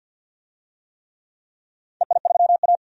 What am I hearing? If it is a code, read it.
Morse code, heard as a keyed tone, about 51 wpm, 700 Hz. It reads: EI3A
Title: DXCC-100 Call Signs - 4 Characters - 50wpm